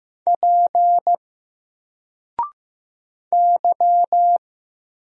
Morse code, heard as a keyed tone, about 15 wpm, 700 Hz. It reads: P Y
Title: Single Character (Letter:Number:Punctuation:Prosign) 15wpm